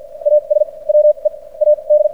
Morse code, heard as a keyed tone, about 56 wpm, spaced as for 39 wpm, 600 Hz. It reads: AS WE AD
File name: archived / CRSnet